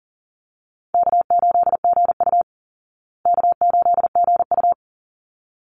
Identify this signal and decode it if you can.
Morse code, heard as a keyed tone, about 40 wpm, 700 Hz. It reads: K8GU K8GU
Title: Easy Call Signs Repeated 2x - 40wpm